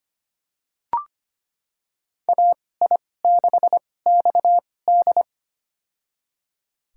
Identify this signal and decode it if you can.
Morse code, heard as a keyed tone, about 25 wpm, spaced as for 18 wpm, 700 Hz. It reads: AI6XD
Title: USA Call Signs - 5 Characters - 2x Character Spacing - 25wpm